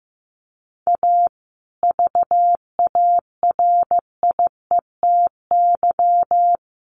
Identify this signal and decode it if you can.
Morse code, heard as a keyed tone, about 15 wpm, 700 Hz. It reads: A VARIETY
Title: Sets of 3 Words 15wpm